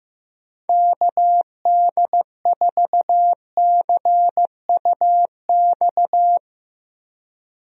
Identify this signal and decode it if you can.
Morse code, heard as a keyed tone, about 15 wpm, 700 Hz. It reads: KD4CUX
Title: USA Call Signs - 6 Characters - Repeated 2x - 15wpm